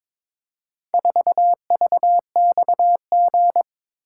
Morse code, heard as a keyed tone, about 22 wpm, 690 Hz. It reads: 4VXG
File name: practice / 22wpm / Four Letter-Number Combinations - 22wpm